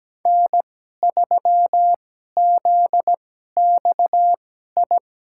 Morse code, heard as a keyed tone, about 17 wpm, spaced as for 12 wpm, 700 Hz. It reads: N3ZXI
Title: USA Call Signs - 5 Characters - 2x Character Spacing - 17wpm